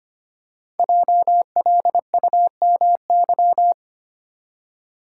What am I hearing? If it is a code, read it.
Morse code, heard as a keyed tone, about 25 wpm, 700 Hz. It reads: JLUMY
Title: Code Groups - 5 Letters - 25wpm